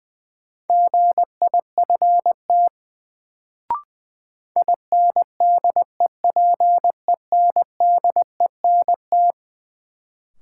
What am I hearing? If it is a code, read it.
Morse code, heard as a keyed tone, about 21 wpm, 700 Hz. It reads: GIFT INDEPENDENT
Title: Sentences from Top 900 Words - New Word Drill - 20wpm